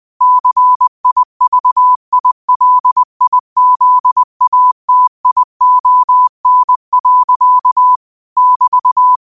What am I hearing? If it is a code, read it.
Morse code, heard as a keyed tone, about 20 wpm, 1000 Hz. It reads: CIVILIZATION. <BT>
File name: qotdfiles / 20wpm